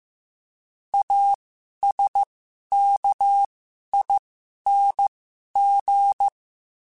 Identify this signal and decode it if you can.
Morse code, heard as a keyed tone, about 15 wpm, spaced as for 11 wpm, 800 Hz. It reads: ASKING
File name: Special INGs-mp3s / Special INGs-mp3s / Sngl Ing @10